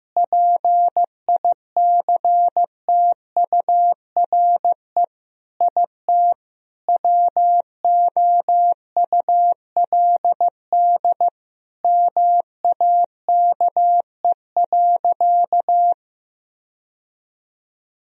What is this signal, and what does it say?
Morse code, heard as a keyed tone, about 15 wpm, 700 Hz. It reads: PICTURE IT WOULD MAKE.
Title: Sentences from Top 1000 Words - Review Part 1 - 15wpm